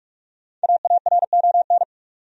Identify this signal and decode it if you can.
Morse code, heard as a keyed tone, about 48 wpm, 700 Hz. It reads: AARON
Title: USA Common Names - Max 5 Letters - 45wpm